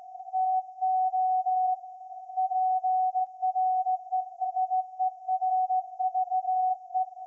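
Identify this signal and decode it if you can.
Morse code, heard as a keyed tone, about 17 wpm, 750 Hz. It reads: TO PRESERVE
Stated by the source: Morse code from SAQ, received at a station in Putignano, Apulia, Italy, on VLF.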